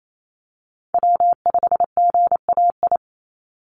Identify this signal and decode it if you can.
Morse code, heard as a keyed tone, about 28 wpm, 700 Hz. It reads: W5GAI
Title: Call Signs Repeated 2x - 28wpm